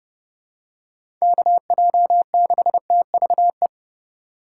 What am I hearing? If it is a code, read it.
Morse code, heard as a keyed tone, about 30 wpm, 700 Hz. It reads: KJ6TVE
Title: USA Call Signs - 2x Character Spacing - 30wpm